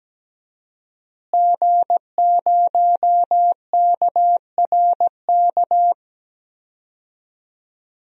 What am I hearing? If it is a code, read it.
Morse code, heard as a keyed tone, about 17 wpm, 700 Hz. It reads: G0KRK
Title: DXCC-100 Call Signs - 5 Characters - Repeated 3x - 17wpm